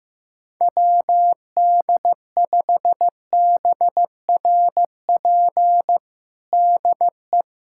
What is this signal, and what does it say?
Morse code, heard as a keyed tone, about 16 wpm, 700 Hz. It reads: WD5BRP DE
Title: USA Callsign DE Callsign 15wpm